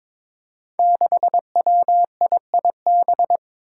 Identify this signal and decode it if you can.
Morse code, heard as a keyed tone, about 22 wpm, 700 Hz. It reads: 6WIIB
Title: Five Challenging Letter-Number Combinations 22wpm